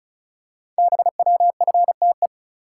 Morse code, heard as a keyed tone, about 36 wpm, 700 Hz. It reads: BWFTE
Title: Code Groups - 5 Letters - 35wpm